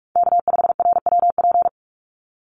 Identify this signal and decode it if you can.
Morse code, heard as a keyed tone, about 44 wpm, 700 Hz. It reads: K5RWP
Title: NA SOTA Chaser Callsigns 45wpm